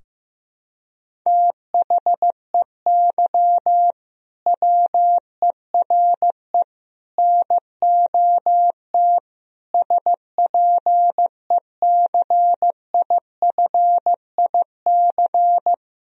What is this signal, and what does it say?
Morse code, heard as a keyed tone, about 15 wpm, 700 Hz. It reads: THEY WERE NOT SPECIFIC
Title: Sentences from Top 1000 Words - Review Part 3 - 15wpm